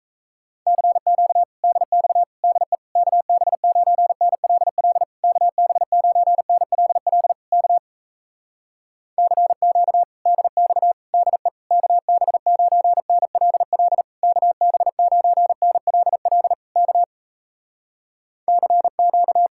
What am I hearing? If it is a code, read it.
Morse code, heard as a keyed tone, about 37 wpm, 700 Hz. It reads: CQ DX DE KB9NLL KB9NLL K CQ DX DE KB9NLL KB9NLL K CQ